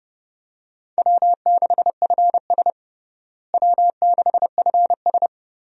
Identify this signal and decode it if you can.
Morse code, heard as a keyed tone, about 30 wpm, 700 Hz. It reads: W6FS W6FS
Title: USA Call Signs - 4 Characters - Repeated 2x - 30wpm